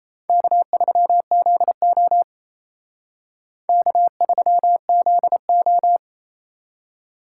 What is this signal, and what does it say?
Morse code, heard as a keyed tone, about 32 wpm, 700 Hz. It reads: K3ZO K3ZO
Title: Speed Racing - USA 1x2 Call Signs 28wpm